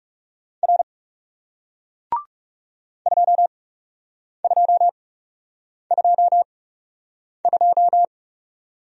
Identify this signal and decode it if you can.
Morse code, heard as a keyed tone, about 40 wpm, spaced as for 8 wpm, 700 Hz. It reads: R 2222